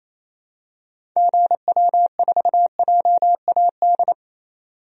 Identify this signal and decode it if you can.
Morse code, heard as a keyed tone, about 28 wpm, 700 Hz. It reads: GW4JAD
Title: DXCC-100 Call Signs - 6 Characters - 28wpm